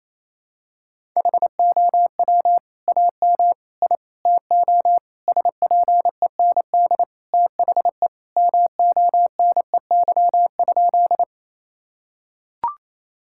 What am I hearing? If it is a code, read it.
Morse code, heard as a keyed tone, about 28 wpm, 700 Hz. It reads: HOW AM I TO SPEND THE MONEY?
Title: Sentences from Top 1000 Words - Review Part 5 - 28wpm